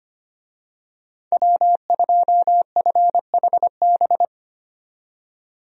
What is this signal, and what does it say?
Morse code, heard as a keyed tone, about 25 wpm, 700 Hz. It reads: W2FHB
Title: USA Call Signs - 5 Characters - 25wpm